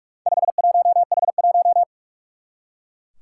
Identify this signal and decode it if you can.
Morse code, heard as a keyed tone, about 44 wpm, 700 Hz. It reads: 51H1
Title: Four Challenging Letter-Number Combinations 45wpm